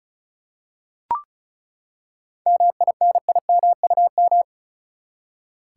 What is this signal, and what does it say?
Morse code, heard as a keyed tone, about 36 wpm, 700 Hz. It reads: MINIMUM
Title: Seven Letter Words - 35wpm